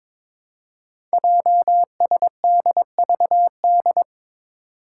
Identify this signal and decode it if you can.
Morse code, heard as a keyed tone, about 22 wpm, 700 Hz. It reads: JSDVD
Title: Code Groups - 5 Letters - 22wpm